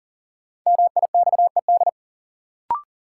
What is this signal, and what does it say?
Morse code, heard as a keyed tone, about 40 wpm, 700 Hz. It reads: MIXED
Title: Sentences from Top 1500 Words - New Word Drill - 40wpm